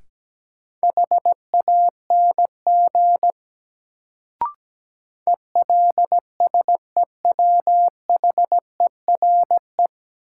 Morse code, heard as a keyed tone, about 17 wpm, 700 Hz. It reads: HANG ELSEWHERE
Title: Sentences from Top 1600 Words - New Word Drill - 17wpm